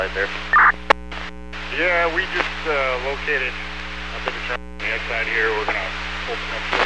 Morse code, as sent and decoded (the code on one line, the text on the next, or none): none